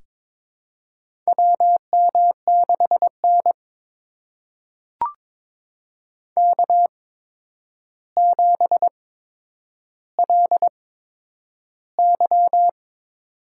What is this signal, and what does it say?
Morse code, heard as a keyed tone, about 22 wpm, spaced as for 9 wpm, 700 Hz. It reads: WM6N K7LY
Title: USA Call Signs - 4 Characters - 8x Character Spacing - 22wpm